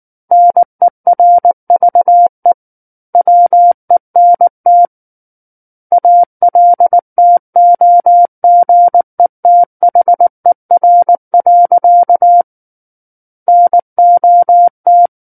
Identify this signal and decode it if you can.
Morse code, heard as a keyed tone, about 19 wpm, 700 Hz. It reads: NERVE WENT ALTOGETHER. NOT
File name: Book One / 10-Chapter 10 - 19 WPM